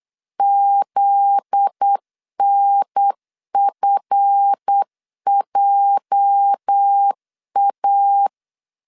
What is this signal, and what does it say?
Morse code, heard as a keyed tone, about 8 wpm, 800 Hz. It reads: ZNFJA
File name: LEZIONI / 8 YJ45 / 8 YJ 8wpm 3313